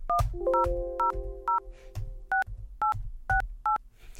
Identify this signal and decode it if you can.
DTMF, heard as telephone touch tones, tones of about 140 ms. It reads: 10006868